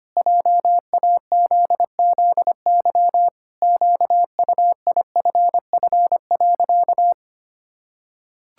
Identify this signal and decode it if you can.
Morse code, heard as a keyed tone, about 25 wpm, 700 Hz. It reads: JAZZY QUIFF.
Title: Pangram Challenge v2 - 25wpm